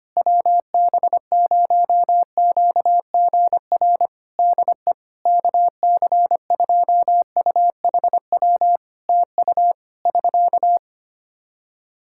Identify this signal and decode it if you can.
Morse code, heard as a keyed tone, about 25 wpm, 700 Hz. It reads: WB0QGR DE KC2UHW TU <SK>
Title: Practice QSO - Exchange 4 - Ending - 25wpm